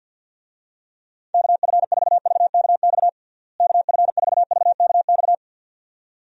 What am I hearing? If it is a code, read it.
Morse code, heard as a keyed tone, about 50 wpm, 700 Hz. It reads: KF4VKX KF4VKX
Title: USA Call Signs - 6 Characters - Repeated 2x - 50wpm